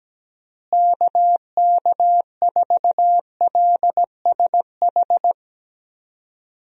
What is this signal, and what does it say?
Morse code, heard as a keyed tone, about 17 wpm, 700 Hz. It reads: KK4LSH
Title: USA Call Signs - 6 Characters - Repeated 3x - 17wpm